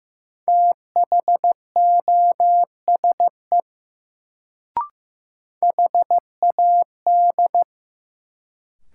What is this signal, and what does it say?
Morse code, heard as a keyed tone, about 15 wpm, 700 Hz. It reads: THOSE HAD
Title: Sets of 2 Words - New Word Drill - 15wpm